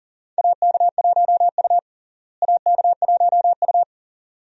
Morse code, heard as a keyed tone, about 40 wpm, 700 Hz. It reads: AK1U AK1U
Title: USA Call Signs - 4 Characters - Repeated 2x - 40wpm